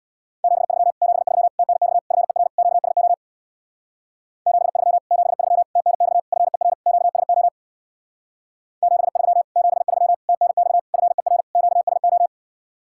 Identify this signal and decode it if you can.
Morse code, heard as a keyed tone, about 64 wpm, 700 Hz. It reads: 73 73 TNX FER QSO 73 73 TNX FER QSO 73 73 TNX FER QSO